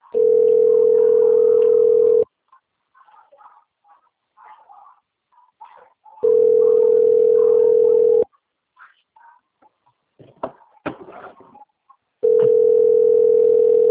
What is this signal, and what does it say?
Ringback tone (North America), 440 + 480 Hz.